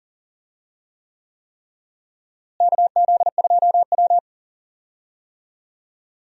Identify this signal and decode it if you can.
Morse code, heard as a keyed tone, about 40 wpm, 700 Hz. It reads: KZ2W
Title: USA Call Signs - 4 Characters - Repeated 3x - 40wpm